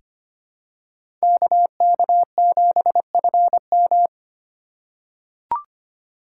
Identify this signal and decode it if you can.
Morse code, heard as a keyed tone, about 25 wpm, 700 Hz. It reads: KK7FM
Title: USA Call Signs - 5 Characters - 4x Character Spacing - 25wpm